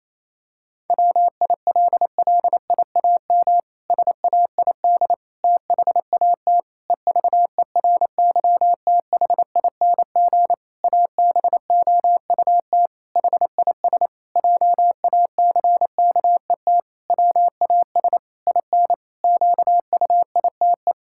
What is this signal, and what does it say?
Morse code, heard as a keyed tone, about 28 wpm, 700 Hz. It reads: WILLIAM SAID THAT EVERYTHING ABOUT HIS JACKET WAS IN QUITE
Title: Pangram Challenge v2 - 28wpm